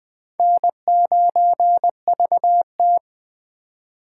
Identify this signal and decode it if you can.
Morse code, heard as a keyed tone, about 25 wpm, 700 Hz. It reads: N9VT
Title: Speed Racing - USA 1x2 Call Signs 20wpm